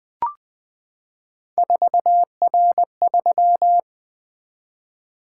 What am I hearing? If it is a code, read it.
Morse code, heard as a keyed tone, about 20 wpm, 700 Hz. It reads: 4R3
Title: Three Letter-Number Combinations - 20wpm